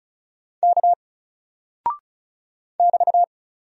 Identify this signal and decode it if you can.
Morse code, heard as a keyed tone, about 36 wpm, 700 Hz. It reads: K <BT>
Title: Single Character (Letter:Number:Punctuation:Prosign) 35wpm